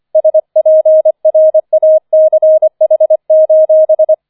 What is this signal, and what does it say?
Morse code, heard as a keyed tone, about 25 wpm, 630 Hz. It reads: SPRACH: